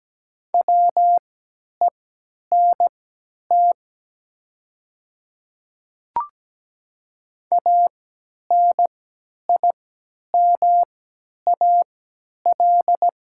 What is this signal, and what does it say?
Morse code, heard as a keyed tone, about 17 wpm, spaced as for 10 wpm, 700 Hz. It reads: WENT ANIMAL
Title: 3x Character Spacing - Top 200 Words - 17wpm